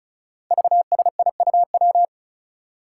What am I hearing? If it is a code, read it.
Morse code, heard as a keyed tone, about 36 wpm, 700 Hz. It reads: VSIUW
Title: Five Challenging Letter-Number Combinations 35wpm